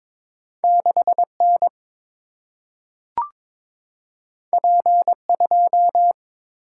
Morse code, heard as a keyed tone, about 22 wpm, 700 Hz. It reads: 6 N P 2
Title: Single-Letter Single-Number 22wpm